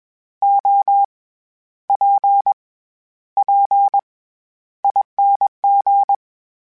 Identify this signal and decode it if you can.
Morse code, heard as a keyed tone, about 21 wpm, spaced as for 11 wpm, 800 Hz. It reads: OPPING